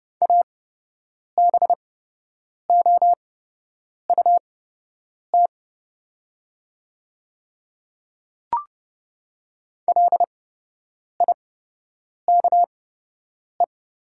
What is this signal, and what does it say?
Morse code, heard as a keyed tone, about 30 wpm, spaced as for 8 wpm, 700 Hz. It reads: ABOUT LIKE
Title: 8x Character Spacing - Top 100 Words - 30wpm